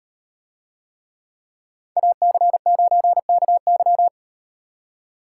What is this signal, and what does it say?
Morse code, heard as a keyed tone, about 38 wpm, 700 Hz. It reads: AC9KY